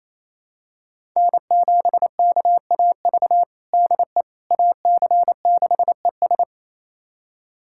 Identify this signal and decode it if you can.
Morse code, heard as a keyed tone, about 28 wpm, 700 Hz. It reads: N7KAV DE AC6ES
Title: USA Callsign DE Callsign - 5 Characters - 28wpm